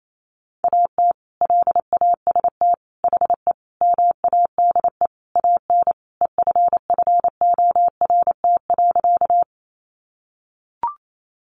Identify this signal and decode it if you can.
Morse code, heard as a keyed tone, about 28 wpm, 700 Hz. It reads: AT LAST HE MADE AN EFFORT.
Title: Sentences from Top 1200 Words 28wpm